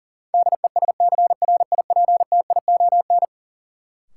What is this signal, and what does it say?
Morse code, heard as a keyed tone, about 40 wpm, 700 Hz. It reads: DESCRIPTION
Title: Sentences from Top 1300 Words - New Word Drill - 40wpm